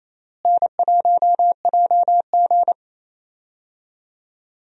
Morse code, heard as a keyed tone, about 28 wpm, 700 Hz. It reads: N1JG
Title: USA Call Signs - 4 Characters - Repeated 3x - 28wpm